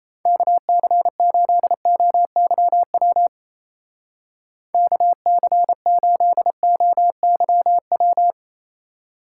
Morse code, heard as a keyed tone, about 34 wpm, 700 Hz. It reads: KC8OYW KC8OYW
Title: Speed Racing - USA 2x3 Call Signs 28wpm